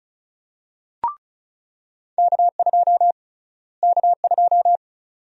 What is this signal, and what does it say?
Morse code, heard as a keyed tone, about 34 wpm, 700 Hz. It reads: K2 K2
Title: USA Call Sign Prefixes - Repeated 2x - 35wpm